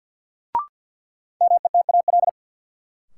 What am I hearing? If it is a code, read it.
Morse code, heard as a keyed tone, about 53 wpm, 700 Hz. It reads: METAL
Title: Sentences from Top 1000 Words - Review Part 6 - New Word Drill - 50wpm